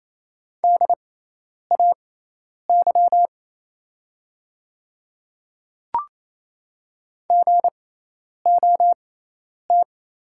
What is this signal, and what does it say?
Morse code, heard as a keyed tone, about 28 wpm, spaced as for 10 wpm, 700 Hz. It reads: DAY GOT
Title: 6x Character Spacing - Top 500 Words - 28wpm